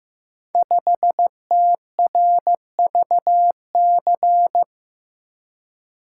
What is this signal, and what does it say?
Morse code, heard as a keyed tone, about 15 wpm, 700 Hz. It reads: 5TRVC
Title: Code Groups - 5 Letter:Numbers - 15wpm